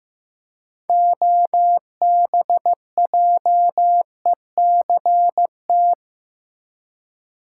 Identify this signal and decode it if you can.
Morse code, heard as a keyed tone, about 15 wpm, 700 Hz. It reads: OBJECT
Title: Six Letter Words - 15wpm